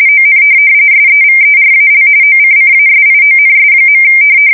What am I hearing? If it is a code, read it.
RTTY, about 45.45 baud. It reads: ON SEVERAL OCCASIONS.
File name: fsk170 / USS-New-Jersey